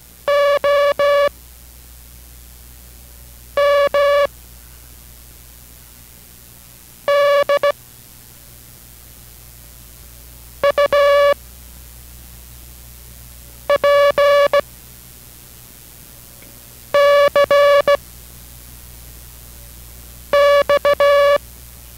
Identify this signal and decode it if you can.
Morse code, heard as a keyed tone, about 16 wpm, 570 Hz. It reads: O M D U P C X